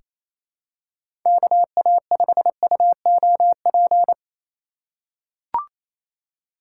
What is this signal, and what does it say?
Morse code, heard as a keyed tone, about 28 wpm, 700 Hz. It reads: KA5UOP
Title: USA Call Signs - 6 Characters - 2x Character Spacing - 28wpm